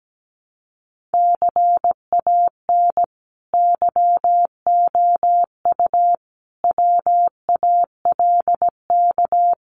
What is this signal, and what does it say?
Morse code, heard as a keyed tone, about 17 wpm, 700 Hz. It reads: CAN YOU WALK
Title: Sentences from Top 1000 Words 17wpm